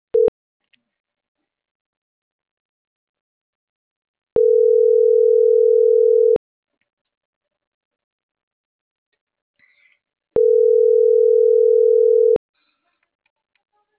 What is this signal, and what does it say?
Ringback tone (North America), 440 + 480 Hz.